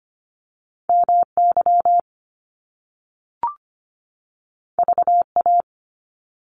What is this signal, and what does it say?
Morse code, heard as a keyed tone, about 25 wpm, 700 Hz. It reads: MY VA
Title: Top Two Letter Combinations in Words 25wpm